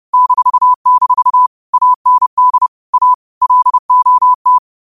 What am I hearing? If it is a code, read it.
Morse code, heard as a keyed tone, about 30 wpm, 1000 Hz. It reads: -- AND A LOT